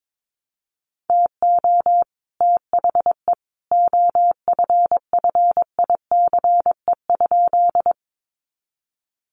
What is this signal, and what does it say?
Morse code, heard as a keyed tone, about 22 wpm, 700 Hz. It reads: TO THE OFFICE?
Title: Sentences from Top 1000 Words 22wpm